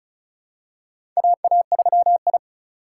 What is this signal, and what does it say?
Morse code, heard as a keyed tone, about 34 wpm, 700 Hz. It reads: AA3I
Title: USA Call Signs - 4 Characters - 2x Character Spacing - 35wpm